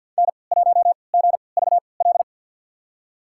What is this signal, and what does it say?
Morse code, heard as a keyed tone, about 52 wpm, spaced as for 29 wpm, 690 Hz. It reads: N1GVL
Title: USA Call Signs - 3x Character Spacing - 50wpm